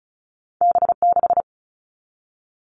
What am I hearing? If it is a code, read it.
Morse code, heard as a keyed tone, about 36 wpm, 700 Hz. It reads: B6
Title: All Cut Numbers - 2-Digit 35wpm